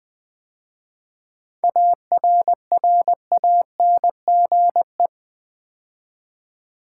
Morse code, heard as a keyed tone, about 20 wpm, 700 Hz. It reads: ARRANGE
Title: 7 Letter Words - First 3 Letters Given - 20wpm